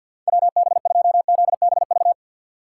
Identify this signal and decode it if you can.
Morse code, heard as a keyed tone, about 50 wpm, 700 Hz. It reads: WB2ZBV
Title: USA Call Signs - 6 Characters - 6x Character Spacing - 50wpm